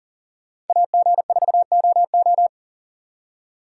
Morse code, heard as a keyed tone, about 53 wpm, 700 Hz. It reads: AG4OO